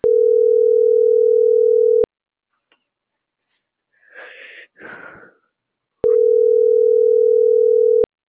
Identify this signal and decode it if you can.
Ringback tone (North America), 440 + 480 Hz.
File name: CENTRAL MABE